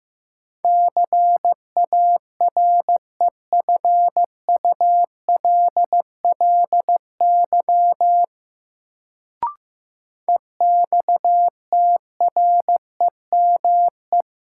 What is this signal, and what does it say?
Morse code, heard as a keyed tone, about 15 wpm, 700 Hz. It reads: CAREFULLY EXTREME
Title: Sentences from Top 1300 Words - New Word Drill Alt - 15wpm